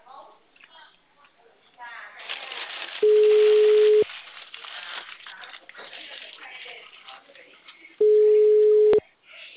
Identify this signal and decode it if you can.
Ringback tone (Europe, 425 Hz), with speech.
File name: out- -2013- - - .1411